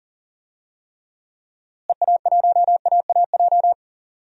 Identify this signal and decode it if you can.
Morse code, heard as a keyed tone, about 40 wpm, 700 Hz. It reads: EA1AAJ
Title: DXCC-100 Call Signs - 6 Characters - Repeated 2x - 40wpm